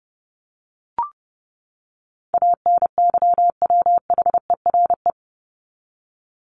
Morse code, heard as a keyed tone, about 30 wpm, 700 Hz. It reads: ANYWHERE
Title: Top 1251-1500 Words 30wpm